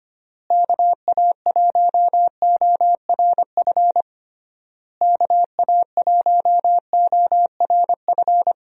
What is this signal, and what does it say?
Morse code, heard as a keyed tone, about 25 wpm, 700 Hz. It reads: KA1ORF KA1ORF
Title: USA Call Signs - 6 Characters - Repeated 3x - 25wpm